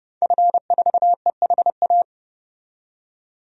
Morse code, heard as a keyed tone, about 30 wpm, 700 Hz. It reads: F4EHA
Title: DXCC-100 Call Signs - 5 Characters - 30wpm